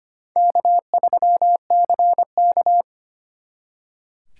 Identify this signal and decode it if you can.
Morse code, heard as a keyed tone, about 29 wpm, 700 Hz. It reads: K3CK
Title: Speed Racing - USA 1x2 Call Signs 25wpm